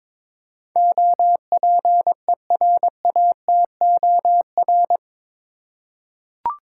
Morse code, heard as a keyed tone, about 22 wpm, 700 Hz. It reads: OPERATOR
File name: practice / 22wpm / Eight Letter Words - 22wpm